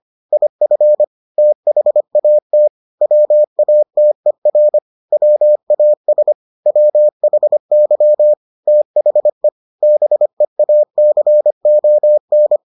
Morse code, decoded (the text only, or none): IF THAT WATER WAS WHY THE BEACON